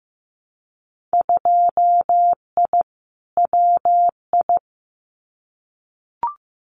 Morse code, heard as a keyed tone, about 15 wpm, 700 Hz. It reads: 2I WI
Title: Winter Field Day Exchanges 15wpm